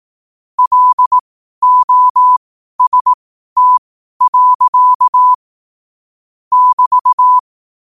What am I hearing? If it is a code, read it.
Morse code, heard as a keyed tone, about 18 wpm, spaced as for 12 wpm, 1000 Hz. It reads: LOST. <BT>